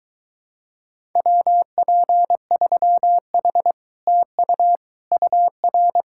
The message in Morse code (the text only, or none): WP3H TU UR